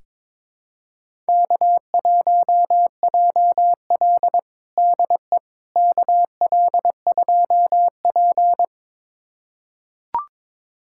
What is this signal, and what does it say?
Morse code, heard as a keyed tone, about 22 wpm, 700 Hz. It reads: K1JL DE KL2P
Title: USA Callsign DE Callsign - 4 Characters - 22wpm